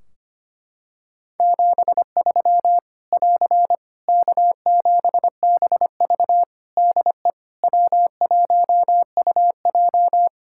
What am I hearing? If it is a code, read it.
Morse code, heard as a keyed tone, about 25 wpm, 700 Hz. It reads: 73 <AR> K7BV DE W1UJ
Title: K1USN SST QSO Practice 25wpm